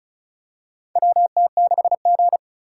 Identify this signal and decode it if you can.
Morse code, heard as a keyed tone, about 36 wpm, 700 Hz. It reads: WT6G
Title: USA Call Signs - 4 Characters - 4x Character Spacing - 35wpm